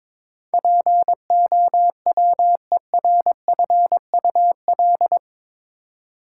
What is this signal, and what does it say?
Morse code, heard as a keyed tone, about 21 wpm, 700 Hz. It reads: POWERFUL
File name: practice / 22wpm / Eight Letter Words - 22wpm